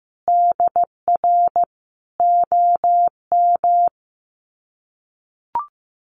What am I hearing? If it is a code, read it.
Morse code, heard as a keyed tone, about 15 wpm, 700 Hz. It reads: DR OM
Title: Common QSO Phrases 15wpm